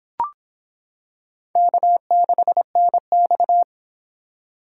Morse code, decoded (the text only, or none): K6NX